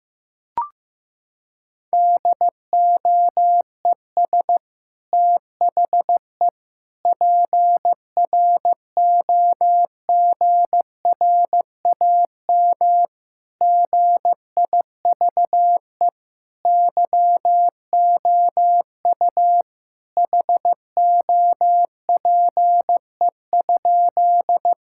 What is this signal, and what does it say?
Morse code, heard as a keyed tone, about 15 wpm, 700 Hz. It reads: DOES THE PROGRAM GIVE YOU HOPE?
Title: Sentences from Top 1000 Words - Review Part 5 - 15wpm